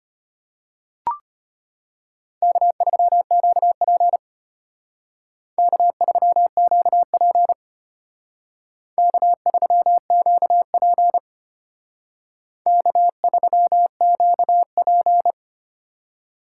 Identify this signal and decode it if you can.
Morse code, heard as a keyed tone, about 31 wpm, 700 Hz. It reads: K3QP K3QP K3QP K3QP
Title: Speed Racing - USA 1x2 Call Signs 25wpm